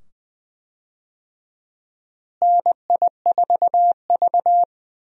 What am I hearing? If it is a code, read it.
Morse code, heard as a keyed tone, about 20 wpm, 700 Hz. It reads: NI4V
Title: USA Call Signs - 4 Characters - Repeated 2x - 20wpm